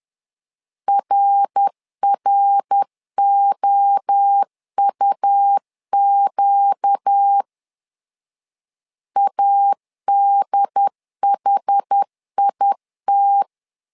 Morse code, heard as a keyed tone, about 10 wpm, 800 Hz. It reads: RROUQ ADHIT
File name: LEZIONI / 4 Q 10wpm 2210